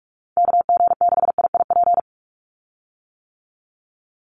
Morse code, heard as a keyed tone, about 44 wpm, 700 Hz. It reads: KG6IIP
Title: Call Signs - 45wpm